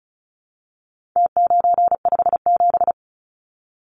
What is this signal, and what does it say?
Morse code, heard as a keyed tone, about 34 wpm, 700 Hz. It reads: T957
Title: Numbers - Contest Style - 4-Digit 35wpm